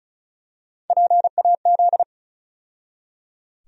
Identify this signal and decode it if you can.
Morse code, heard as a keyed tone, about 36 wpm, 700 Hz. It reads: PAZ
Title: USA Call Sign Postfixes - 3 Letters 35wpm